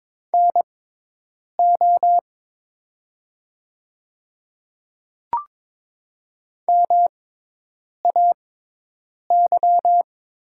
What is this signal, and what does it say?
Morse code, heard as a keyed tone, about 22 wpm, spaced as for 8 wpm, 700 Hz. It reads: NO MAY
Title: 6x Character Spacing - Top 100 Words - 22wpm